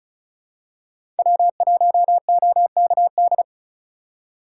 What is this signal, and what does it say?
Morse code, heard as a keyed tone, about 34 wpm, 700 Hz. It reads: W1OKD
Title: USA Call Signs - 5 Characters - 6x Character Spacing - 35wpm